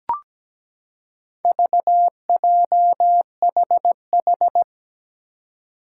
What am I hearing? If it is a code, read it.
Morse code, heard as a keyed tone, about 17 wpm, 700 Hz. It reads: VJHH
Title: Four Challenging Letter-Number Combinations 17wpm